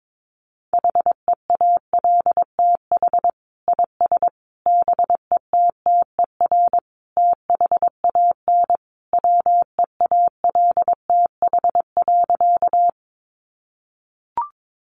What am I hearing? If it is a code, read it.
Morse code, heard as a keyed tone, about 22 wpm, 700 Hz. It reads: HEALTH IS BETTER THAN WEALTH.
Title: Sentences from Top 1200 Words 22wpm